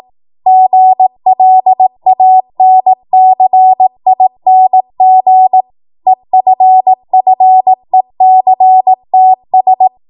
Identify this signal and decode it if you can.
Morse code, heard as a keyed tone, about 18 wpm, 750 Hz. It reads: GLANCING EFFECTS
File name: Morse / 18WPM